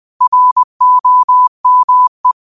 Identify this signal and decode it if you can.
Morse code, heard as a keyed tone, about 20 wpm, 1000 Hz. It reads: ROME
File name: qotdfiles / 20wpm